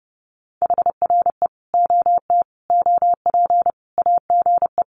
Morse code, heard as a keyed tone, about 30 wpm, 700 Hz. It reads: HRE OT OP AGE
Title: Common QSO Phrases 30wpm